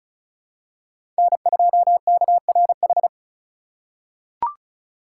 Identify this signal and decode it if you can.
Morse code, heard as a keyed tone, about 34 wpm, 700 Hz. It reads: N2KRH
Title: USA Call Signs - 4x Character Spacing - 35wpm